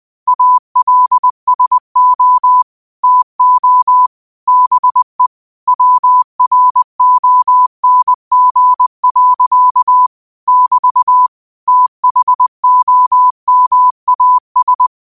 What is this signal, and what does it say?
Morse code, heard as a keyed tone, about 20 wpm, 1000 Hz. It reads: ALSO TO BE WRONG. <BT> THOMAS